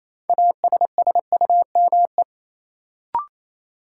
Morse code, heard as a keyed tone, about 27 wpm, 700 Hz. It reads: ASSUME
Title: Sentences from Top 1500 Words - New Word Drill - 28wpm